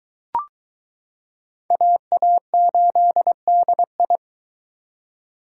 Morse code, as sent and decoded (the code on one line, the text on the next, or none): .- .- ---.. -.. ..
AA8DI